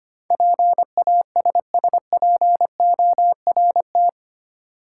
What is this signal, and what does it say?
Morse code, heard as a keyed tone, about 25 wpm, 700 Hz. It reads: PASSPORT
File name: Eight Letter Words - 25wpm